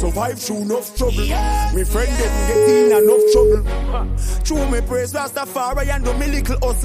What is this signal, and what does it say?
Ringback tone (Europe, 425 Hz), with noise.